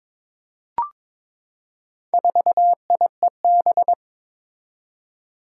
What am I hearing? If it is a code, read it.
Morse code, heard as a keyed tone, about 22 wpm, 700 Hz. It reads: 4IEB
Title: Four Challenging Letter-Number Combinations 22wpm